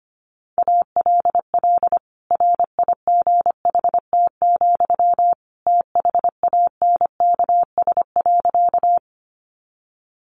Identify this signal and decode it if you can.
Morse code, heard as a keyed tone, about 25 wpm, 700 Hz. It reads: ALL RIGHT, THANKS.
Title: Sentences from Top 600 Words 25wpm